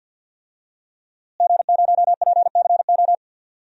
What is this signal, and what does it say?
Morse code, heard as a keyed tone, about 46 wpm, 700 Hz. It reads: G0PCO
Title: DXCC-100 Call Signs - 5 Characters - Repeated 3x - 50wpm